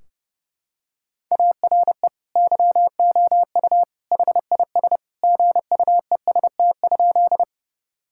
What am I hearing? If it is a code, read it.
Morse code, heard as a keyed tone, about 32 wpm, 700 Hz. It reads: ARE YOU HIS GUEST?
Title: Sentences from Top 1600 Words 30wpm